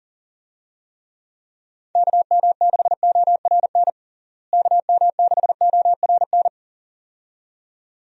Morse code, heard as a keyed tone, about 37 wpm, 700 Hz. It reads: KM6ORN KM6ORN
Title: USA Call Signs - 6 Characters - Repeated 2x - 40wpm